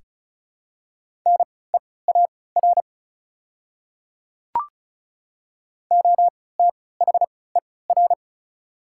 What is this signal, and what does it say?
Morse code, heard as a keyed tone, about 36 wpm, spaced as for 20 wpm, 690 Hz. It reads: NEAR OTHER
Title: 3x Character Spacing - Top 300 Words - 35wpm